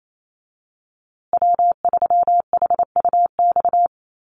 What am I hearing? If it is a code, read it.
Morse code, heard as a keyed tone, about 28 wpm, 700 Hz. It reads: W3HUX
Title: USA Call Signs Repeated 2x 28wpm